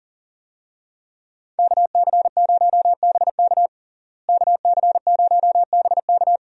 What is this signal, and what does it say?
Morse code, heard as a keyed tone, about 40 wpm, 700 Hz. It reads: KC0BK KC0BK
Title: USA Call Signs - 5 Characters - Repeated 3x - 40wpm